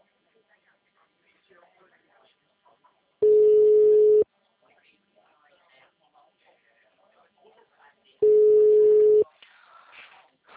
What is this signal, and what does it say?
Ringback tone (Europe, 425 Hz), with speech.